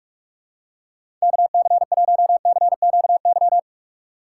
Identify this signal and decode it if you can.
Morse code, heard as a keyed tone, about 46 wpm, 700 Hz. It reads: KC1CQY
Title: USA Call Signs - 6 Characters - 6x Character Spacing - 45wpm